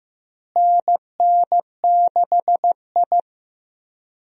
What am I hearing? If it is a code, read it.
Morse code, heard as a keyed tone, about 16 wpm, 700 Hz. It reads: NN6I
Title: USA Call Signs - 4 Characters - Repeated 3x - 15wpm